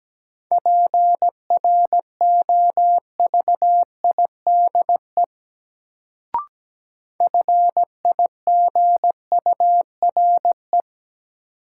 Morse code, heard as a keyed tone, about 17 wpm, 700 Hz. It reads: PROVIDE FIGURE
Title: Sentences from Top 1000 Words - Review Part 2 - New Word Drill - 17wpm